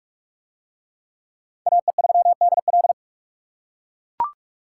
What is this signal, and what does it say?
Morse code, heard as a keyed tone, about 40 wpm, 700 Hz. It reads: AE3DL